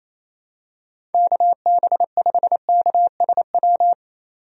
Morse code, heard as a keyed tone, about 28 wpm, 700 Hz. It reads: KB5KSW
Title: USA Call Signs - 6 Characters - 28wpm